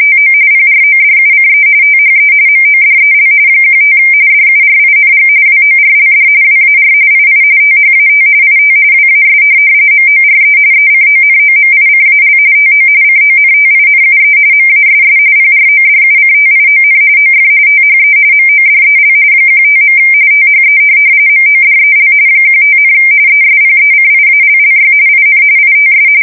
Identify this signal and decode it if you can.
RTTY, about 45.45 baud. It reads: TO PRODUCE THE MODEL 15 DURING WORLD WAR II THAT IT WAS MORE ECONOMICAL TO CONTINUE MASS PRODUCTION OF THE MODEL 15. THE MODEL 15, ESPECIALLY IN ITS
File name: fsk170 / teletype-corp